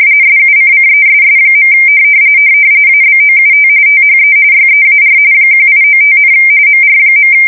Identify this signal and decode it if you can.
RTTY, about 45.45 baud. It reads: SUCCESS. ABOUT THE YEAR 1895, MR.